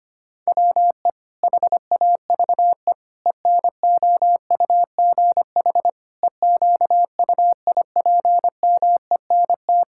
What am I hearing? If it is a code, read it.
Morse code, heard as a keyed tone, about 25 wpm, 700 Hz. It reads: WE HAVE ENOUGH EQUIPMENT
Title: Sentences from Top 1000 Words - Review Part 5 - 25wpm